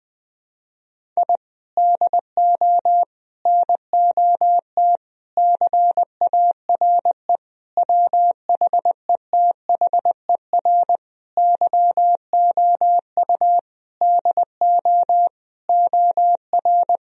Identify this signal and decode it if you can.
Morse code, heard as a keyed tone, about 20 wpm, 700 Hz. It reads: I DO NOT CARE WHETHER YOU DO OR
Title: Sentences from Top 1000 Words - Review Part 3 - 20wpm